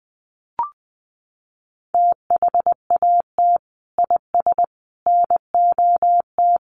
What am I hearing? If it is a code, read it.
Morse code, heard as a keyed tone, about 20 wpm, 700 Hz. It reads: THAT IS NOT
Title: Sentences from Top 600 Words 20wpm